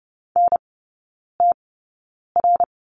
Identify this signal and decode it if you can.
Morse code, heard as a keyed tone, about 30 wpm, spaced as for 9 wpm, 700 Hz. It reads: NTR
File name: Lesson 15e - Number 5 - Three Characters 30wpm